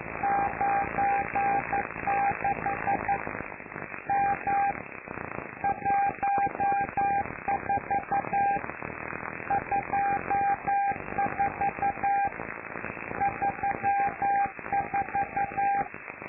Morse code, heard as a keyed tone, about 11 wpm, 800 Hz, with noise. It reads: 96 M 14 24 34
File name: lu4aao / Globo Amsat